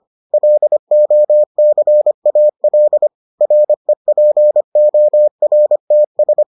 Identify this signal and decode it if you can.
Morse code, heard as a keyed tone, about 25 wpm, 600 Hz. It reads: LOCAL REPORTS